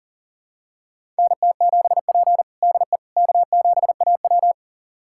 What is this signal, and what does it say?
Morse code, heard as a keyed tone, about 40 wpm, 700 Hz. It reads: NT7P DE K7AW
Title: USA Callsign DE Callsign - 4 Characters - 40wpm